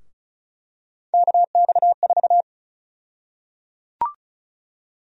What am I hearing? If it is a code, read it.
Morse code, heard as a keyed tone, about 36 wpm, 700 Hz. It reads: KX4
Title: USA Call Sign Prefixes - Repeated 2x - 35wpm